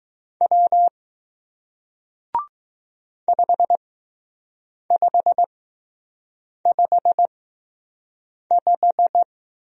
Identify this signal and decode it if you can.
Morse code, heard as a keyed tone, about 20 wpm, spaced as for 6 wpm, 700 Hz. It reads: W 5555